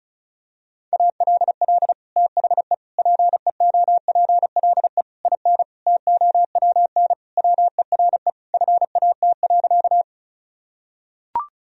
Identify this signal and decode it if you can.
Morse code, heard as a keyed tone, about 36 wpm, 700 Hz. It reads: ALL THE PEOPLE IN TOWN WERE FAT.
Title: Sentences from Top 1000 Words - Review Part 4 - 35wpm